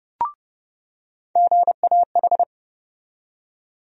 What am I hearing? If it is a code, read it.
Morse code, heard as a keyed tone, about 30 wpm, 700 Hz. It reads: GAH
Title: USA Call Sign Postfixes - 3 Letters 30wpm